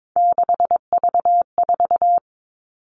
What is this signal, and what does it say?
Morse code, heard as a keyed tone, about 22 wpm, 700 Hz. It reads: 6V4
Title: All Cut Numbers - 3-Digit 22wpm